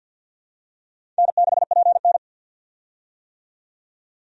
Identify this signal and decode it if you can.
Morse code, heard as a keyed tone, about 52 wpm, 700 Hz. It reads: N6PN
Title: USA Call Signs - 4 Characters - 50wpm